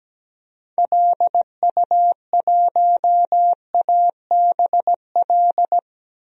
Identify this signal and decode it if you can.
Morse code, heard as a keyed tone, about 17 wpm, 700 Hz. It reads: LU1ABL
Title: DXCC-100 Call Signs - 6 Characters - Repeated 3x - 17wpm